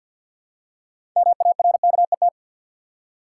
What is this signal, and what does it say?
Morse code, heard as a keyed tone, about 52 wpm, 700 Hz. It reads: MARKET
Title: Six Letter Words - 50wpm